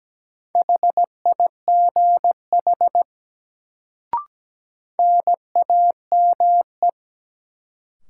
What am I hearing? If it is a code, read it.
Morse code, heard as a keyed tone, about 17 wpm, 700 Hz. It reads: HIGH NAME
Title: Sentences from Top 200 Words - New Word Drill - 17wpm